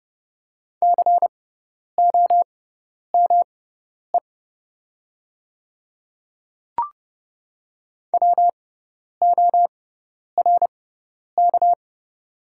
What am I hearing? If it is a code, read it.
Morse code, heard as a keyed tone, about 30 wpm, spaced as for 10 wpm, 700 Hz. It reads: COME WORK
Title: 6x Character Spacing - Top 300 Words - 30wpm